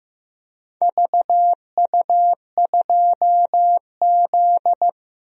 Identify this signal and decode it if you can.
Morse code, heard as a keyed tone, about 15 wpm, 700 Hz. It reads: VU2Z
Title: DXCC-100 Call Signs - 4 Characters - Repeated 3x - 15wpm